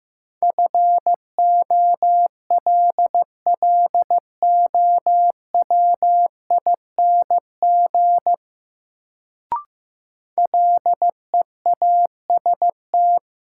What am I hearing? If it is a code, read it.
Morse code, heard as a keyed tone, about 15 wpm, 700 Hz. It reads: FOLLOWING LEAST
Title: Sentences from Top 700 Words - New Word Drill - 15wpm